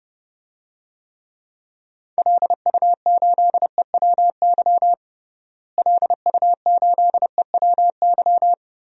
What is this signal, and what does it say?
Morse code, heard as a keyed tone, about 30 wpm, 700 Hz. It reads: LU8EWY LU8EWY
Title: DXCC-100 Call Signs - 6 Characters - Repeated 2x - 30wpm